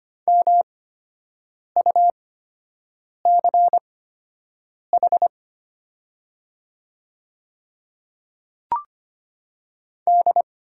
Morse code, heard as a keyed tone, about 25 wpm, spaced as for 7 wpm, 700 Hz. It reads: MUCH D
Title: 8x Character Spacing - Top 200 Words - 25wpm